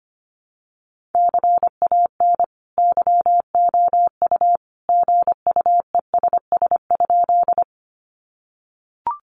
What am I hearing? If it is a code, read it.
Morse code, heard as a keyed tone, about 25 wpm, 700 Hz. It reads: CAN YOU GUESS?
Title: Sentences from Top 1500 Words 25wpm